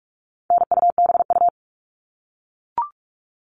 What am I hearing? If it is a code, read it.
Morse code, heard as a keyed tone, about 44 wpm, 700 Hz. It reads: NUBU
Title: All Cut Numbers - 4-Digit 45wpm